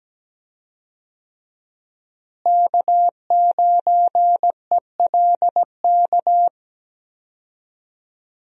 Morse code, heard as a keyed tone, about 17 wpm, 700 Hz. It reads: K9ELK
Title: USA Call Signs - 5 Characters - Repeated 2x - 17wpm